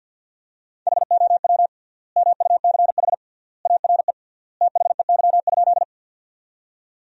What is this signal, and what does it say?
Morse code, heard as a keyed tone, about 48 wpm, 700 Hz. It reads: HOW MUCH ARE THEY?
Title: 3x Word-Spacing - Sentences from Top 200 Words - 50wpm